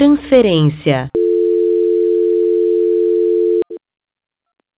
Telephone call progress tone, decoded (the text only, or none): dial tone